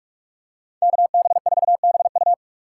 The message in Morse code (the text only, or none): KB4BU